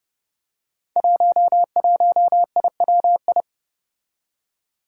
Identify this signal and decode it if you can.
Morse code, heard as a keyed tone, about 30 wpm, 700 Hz. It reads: 11IWI
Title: Five Challenging Letter-Number Combinations 30wpm